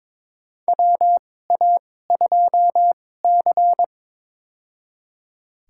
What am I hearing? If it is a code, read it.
Morse code, heard as a keyed tone, about 22 wpm, spaced as for 16 wpm, 700 Hz. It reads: WA2C
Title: USA Call Signs - 4 Characters - 2x Character Spacing - 22wpm